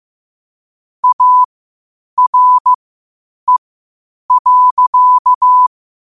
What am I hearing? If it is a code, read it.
Morse code, heard as a keyed tone, about 15 wpm, spaced as for 8 wpm, 1000 Hz. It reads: ARE.
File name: qotdfiles / 8wpm